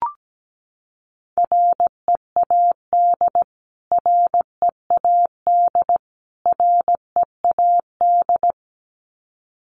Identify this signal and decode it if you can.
Morse code, heard as a keyed tone, about 17 wpm, 700 Hz. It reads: READ READ READ
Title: Top 300 Words Repeated 17wpm